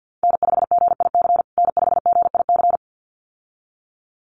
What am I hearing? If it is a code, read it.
Morse code, heard as a keyed tone, about 50 wpm, 700 Hz. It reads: N5GIC N5GIC
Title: USA Call Signs Repeated 2x 50wpm